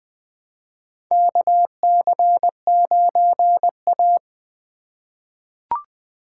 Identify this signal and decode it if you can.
Morse code, heard as a keyed tone, about 20 wpm, 700 Hz. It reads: KC9A
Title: USA Call Signs - 4 Characters - 2x Character Spacing - 20wpm